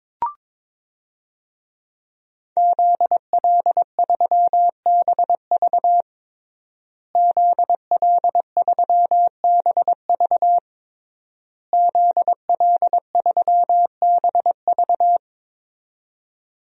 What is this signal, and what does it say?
Morse code, heard as a keyed tone, about 22 wpm, 700 Hz. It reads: ZL3BV ZL3BV ZL3BV
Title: DXCC-100 Call Signs - 5 Characters - Repeated 3x - 22wpm